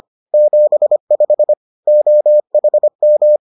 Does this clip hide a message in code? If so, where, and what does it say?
Yes: Morse code from 0.3 to 3.4 s, heard as a keyed tone, about 25 wpm, 600 Hz. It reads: 75 OHM